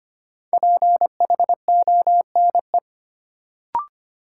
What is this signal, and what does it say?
Morse code, heard as a keyed tone, about 25 wpm, 700 Hz. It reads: PHONE
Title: Sentences from Top 1100 Words - New Word Drill - 25wpm